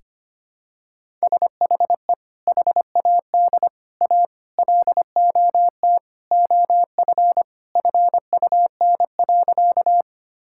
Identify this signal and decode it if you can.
Morse code, heard as a keyed tone, about 25 wpm, 700 Hz. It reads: SHE HAD A LOT OF FUN.
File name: practice / 25wpm / Sentences with 4-Word Phrases - 25wpm